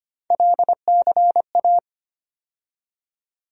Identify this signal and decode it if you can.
Morse code, heard as a keyed tone, about 25 wpm, 700 Hz. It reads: LCA
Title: USA Call Sign Postfixes - 3 Letters 25wpm